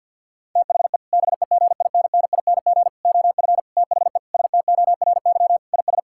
Morse code, heard as a keyed tone, about 50 wpm, 700 Hz. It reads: THE BEGINNING OF THE STORY IS
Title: Sentences from Top 1000 Words - Review Part 6 - 50wpm